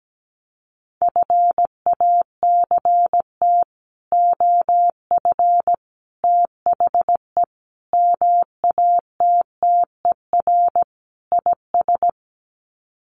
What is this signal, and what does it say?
Morse code, heard as a keyed tone, about 17 wpm, 700 Hz. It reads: FACT OF THE MATTER IS
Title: Sets of 5 Words 17wpm